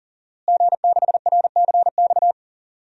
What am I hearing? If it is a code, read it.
Morse code, heard as a keyed tone, about 40 wpm, 700 Hz. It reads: G6RCX
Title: DXCC-100 Call Signs - 5 Characters - Repeated 3x - 40wpm